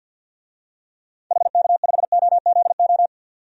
Morse code, heard as a keyed tone, about 47 wpm, 700 Hz. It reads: HK5OZO
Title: DXCC-100 Call Signs - 6 Characters - 50wpm